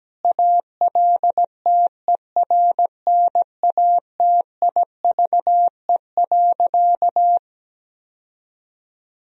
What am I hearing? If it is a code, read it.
Morse code, heard as a keyed tone, about 17 wpm, 700 Hz. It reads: ALTERNATIVE.
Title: Sentences from Top 1000 Words - Review Part 4 - 17wpm